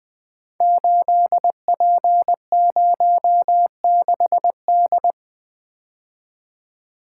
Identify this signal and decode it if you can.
Morse code, heard as a keyed tone, about 20 wpm, 700 Hz. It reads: 8P06D
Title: Code Groups - 5 Letter:Numbers - 20wpm